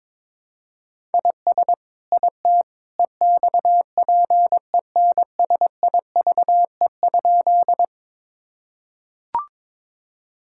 Morse code, heard as a keyed tone, about 22 wpm, 700 Hz. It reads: IS IT EXPENSIVE?
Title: Sentences from Top 1000 Words - Review Part 6 - 22wpm